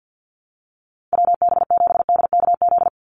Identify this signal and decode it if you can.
Morse code, heard as a keyed tone, about 50 wpm, 700 Hz. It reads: WB7DKZ
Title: Call Signs - 50wpm